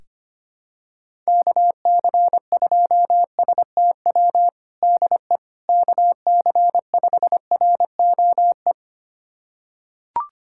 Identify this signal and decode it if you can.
Morse code, heard as a keyed tone, about 25 wpm, 700 Hz. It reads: KC2STW DE KC5ROE
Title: USA Callsign DE Callsign - 6 Characters - 25wpm